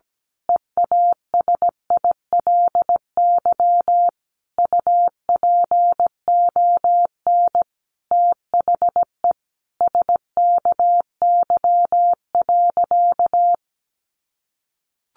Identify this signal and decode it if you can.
Morse code, heard as a keyed tone, about 17 wpm, 700 Hz. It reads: EASILY UPON THE SKY.
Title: A Sound Of Thunder 17wpm